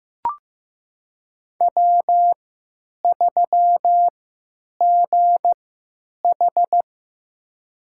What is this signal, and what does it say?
Morse code, heard as a keyed tone, about 15 wpm, spaced as for 9 wpm, 700 Hz. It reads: W3GH
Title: USA Call Signs - 4 Characters - 3x Character Spacing - 15wpm